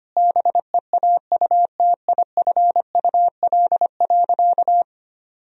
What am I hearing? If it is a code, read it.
Morse code, heard as a keyed tone, about 25 wpm, 700 Hz. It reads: BEAUTIFUL.
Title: Sentences from Top 1000 Words - Review Part 1 - 25wpm